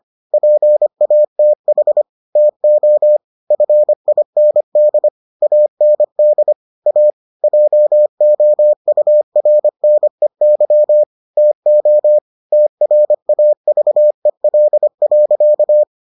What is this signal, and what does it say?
Morse code, heard as a keyed tone, about 25 wpm, 600 Hz. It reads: PATH TO FIND AND A JOURNEY TO TRAVEL.